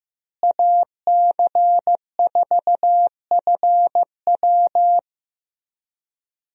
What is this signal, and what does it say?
Morse code, heard as a keyed tone, about 15 wpm, 700 Hz. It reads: AC4FW
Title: USA Call Signs - 5 Characters - Repeated 3x - 15wpm